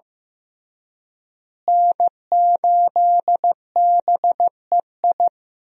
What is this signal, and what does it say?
Morse code, heard as a keyed tone, about 15 wpm, 700 Hz. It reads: N8BEI
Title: USA Call Signs - 5 Characters - Repeated 3x - 15wpm